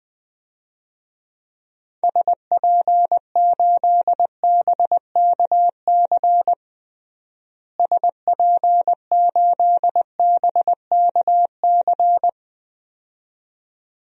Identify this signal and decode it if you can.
Morse code, heard as a keyed tone, about 21 wpm, 700 Hz. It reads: SP8BKC SP8BKC
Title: DXCC-100 Call Signs - 6 Characters - Repeated 2x - 20wpm